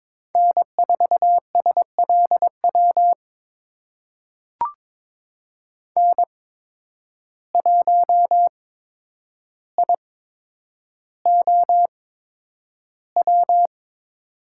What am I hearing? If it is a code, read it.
Morse code, heard as a keyed tone, about 22 wpm, spaced as for 9 wpm, 700 Hz. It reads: N4SLW N1IOW
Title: USA Call Signs - 5 Characters - 8x Character Spacing - 22wpm